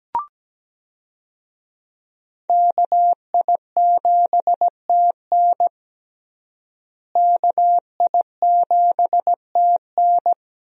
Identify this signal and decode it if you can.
Morse code, heard as a keyed tone, about 17 wpm, 700 Hz. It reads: KI7TN KI7TN
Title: USA Call Signs - 5 Characters - Repeated 2x - 17wpm